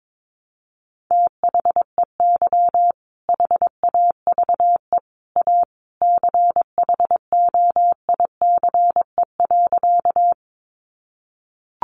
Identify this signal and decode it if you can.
Morse code, heard as a keyed tone, about 22 wpm, 700 Hz. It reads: THEY HAVE A CHOICE.
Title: Sentences from Top 900 Words 22wpm